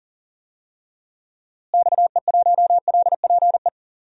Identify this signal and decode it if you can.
Morse code, heard as a keyed tone, about 40 wpm, 700 Hz. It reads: XE1LPE
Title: DXCC-100 Call Signs - 6 Characters - Repeated 3x - 40wpm